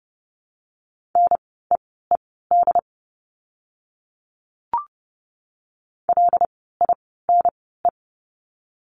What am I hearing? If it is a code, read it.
Morse code, heard as a keyed tone, about 30 wpm, spaced as for 17 wpm, 700 Hz. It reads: NEED LINE
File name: course / 30:10wpm / 115 Lesson 19f - Letter w - Words 30wpm